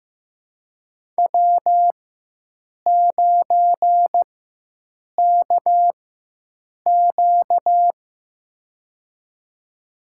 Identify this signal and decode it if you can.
Morse code, heard as a keyed tone, about 15 wpm, spaced as for 7 wpm, 700 Hz. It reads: W9KQ
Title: USA Call Signs - 4 Characters - 4x Character Spacing - 15wpm